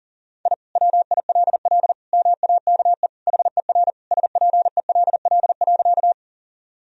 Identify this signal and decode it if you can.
Morse code, heard as a keyed tone, about 37 wpm, 700 Hz. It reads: I WILL MAKE HER SPELL.
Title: Sentences from Top 1000 Words - Review Part 1 - 40wpm